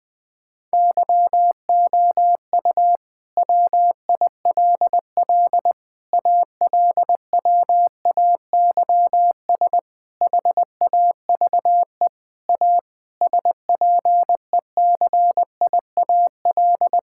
Morse code, heard as a keyed tone, about 20 wpm, 700 Hz. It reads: YOU WILL ALWAYS HAVE A SPECIAL
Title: Sentences with 9-Word Phrases - Spoken Beginning - 20wpm